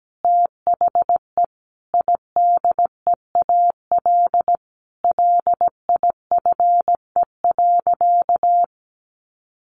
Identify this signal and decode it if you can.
Morse code, heard as a keyed tone, about 17 wpm, 700 Hz. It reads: THE IDEAL LIFE.
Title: Sentences from Top 1400 Words 17wpm